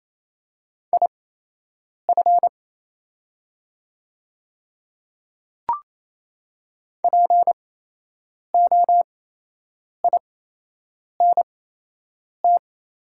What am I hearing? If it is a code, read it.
Morse code, heard as a keyed tone, about 28 wpm, spaced as for 8 wpm, 700 Hz. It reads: IF POINT
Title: 8x Character Spacing - Top 200 Words - 28wpm